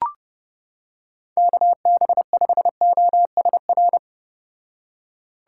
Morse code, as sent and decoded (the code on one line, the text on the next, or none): -.- -... ..... --- ... .-.
KB5OSR